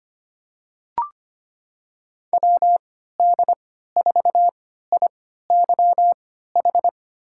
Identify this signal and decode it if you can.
Morse code, heard as a keyed tone, about 25 wpm, spaced as for 14 wpm, 700 Hz. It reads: WD4IYH
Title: USA Call Signs - 3x Character Spacing - 25wpm